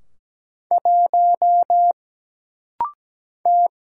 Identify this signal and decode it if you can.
Morse code, heard as a keyed tone, about 17 wpm, 700 Hz. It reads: 1 T
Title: Single Letter-Number - Rapid-Fire 17wpm